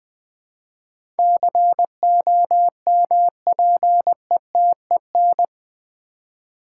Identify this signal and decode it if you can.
Morse code, heard as a keyed tone, about 20 wpm, 700 Hz. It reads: COMPETEN
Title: Build A Word - 4 Letters At A Time - 10-Letter Words - 20wpm